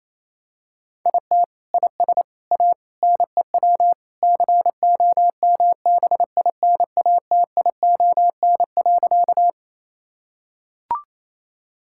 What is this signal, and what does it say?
Morse code, heard as a keyed tone, about 28 wpm, 700 Hz. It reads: IT IS A NEW COMBINATION.
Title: Sentences from Top 1000 Words - Review Part 6 - 28wpm